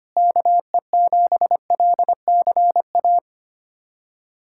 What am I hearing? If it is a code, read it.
Morse code, heard as a keyed tone, about 25 wpm, 700 Hz. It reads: KE7LCA
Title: USA Callsign DE Callsign - 6 Characters - 25wpm